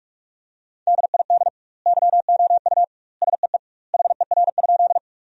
Morse code, heard as a keyed tone, about 46 wpm, 700 Hz. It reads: DID YOU SEE HER?
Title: 2x Word-Spacing - Sentences from Top 100 Words - 45wpm